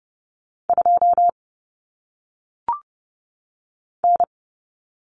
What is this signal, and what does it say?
Morse code, heard as a keyed tone, about 30 wpm, 700 Hz. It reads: 2 N
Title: Common Cut Numbers - 1-Digit 30wpm